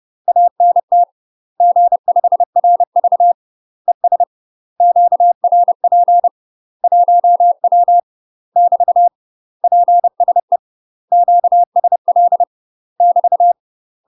Morse code, heard as a keyed tone, about 30 wpm, 700 Hz. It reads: ANT G5RV ES QRP 1W <BT> PSE QSL <BT>
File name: UK MASTER QSOs TRAINING / 30WPM / UK MASTER QSOs 0029